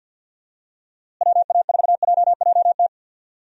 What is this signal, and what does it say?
Morse code, heard as a keyed tone, about 46 wpm, 700 Hz. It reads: WA4JJT
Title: USA Call Signs - 6 Characters - Repeated 2x - 50wpm